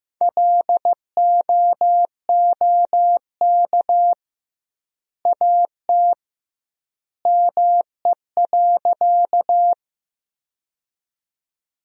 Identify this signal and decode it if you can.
Morse code, heard as a keyed tone, about 15 wpm, 700 Hz. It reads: LOOK AT ME.
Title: 2x Word-Spacing - Sentences from Top 200 Words - 15wpm